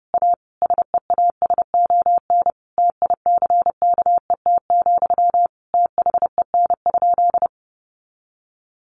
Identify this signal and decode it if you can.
Morse code, heard as a keyed tone, about 30 wpm, 700 Hz. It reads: A SEASON TICKET, THEN?
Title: Sentences from Top 1400 Words 30wpm